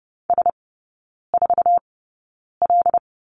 Morse code, heard as a keyed tone, about 30 wpm, spaced as for 9 wpm, 700 Hz. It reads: S4L
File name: course / 30:10wpm / 087 Lesson 15e - Number 5 - Three Characters 30wpm